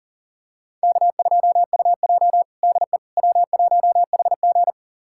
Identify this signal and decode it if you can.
Morse code, heard as a keyed tone, about 40 wpm, 700 Hz. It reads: K2UJ DE W1HG
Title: USA Callsign DE Callsign - 4 Characters - 40wpm